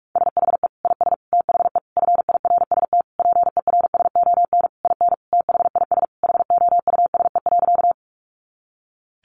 Morse code, heard as a keyed tone, about 44 wpm, 700 Hz. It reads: SHE IS THE FIRST PERSON IN THIS HOUSE.
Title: Sentences from Top 400 Words 45wpm